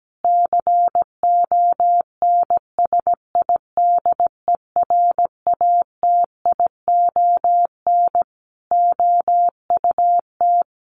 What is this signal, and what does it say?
Morse code, heard as a keyed tone, about 17 wpm, 700 Hz. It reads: CONSIDERATION OUT
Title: Sentences from Top 1500 Words 17wpm